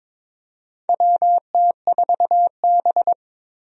Morse code, heard as a keyed tone, about 22 wpm, 700 Hz. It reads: WT4B
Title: USA Call Signs - 4 Characters - Repeated 3x - 22wpm